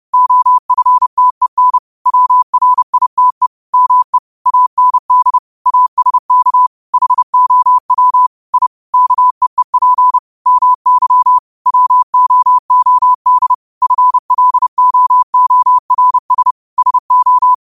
Morse code, heard as a keyed tone, about 30 wpm, 1000 Hz. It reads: OFTEN WRITE ME AND ASK HOW I KEEP MY WOOD FLOORS SO
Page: Quotes for Mon, 28 Apr 2025 in Morse Code at 30 words per minute.